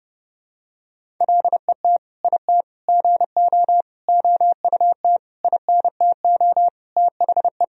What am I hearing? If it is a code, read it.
Morse code, heard as a keyed tone, about 32 wpm, 700 Hz. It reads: LET IT GO OUT INTO THE
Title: Sentences from Top 1000 Words - Review Part 1 - 30wpm